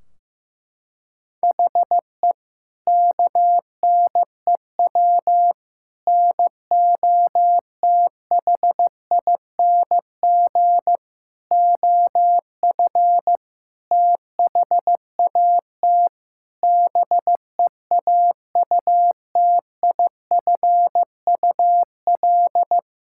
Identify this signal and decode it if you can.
Morse code, heard as a keyed tone, about 15 wpm, 700 Hz. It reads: HE KNEW NOTHING OF THAT BEAUTIFUL
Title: Sentences from Top 1600 Words 15wpm